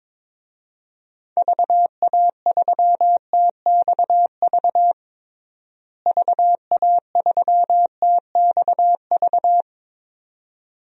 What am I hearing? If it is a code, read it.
Morse code, heard as a keyed tone, about 21 wpm, 700 Hz. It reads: VA3TXV VA3TXV
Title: DXCC-100 Call Signs - 6 Characters - Repeated 2x - 22wpm